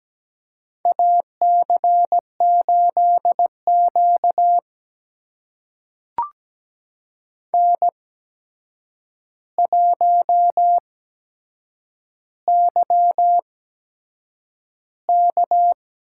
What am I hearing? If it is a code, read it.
Morse code, heard as a keyed tone, about 17 wpm, spaced as for 7 wpm, 700 Hz. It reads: AC8Q N1YK
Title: USA Call Signs - 8x Character Spacing - 17wpm